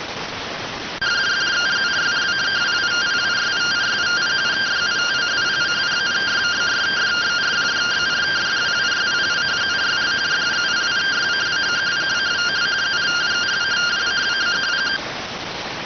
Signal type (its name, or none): two-tone teleprinter signal